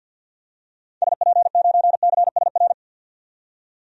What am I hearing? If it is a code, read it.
Morse code, heard as a keyed tone, about 51 wpm, 700 Hz. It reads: SP9CSR
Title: DXCC-100 Call Signs - 6 Characters - 50wpm